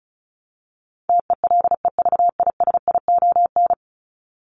Morse code, heard as a keyed tone, about 34 wpm, 700 Hz. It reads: TELEVISION
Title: Top 501-750 Words 35wpm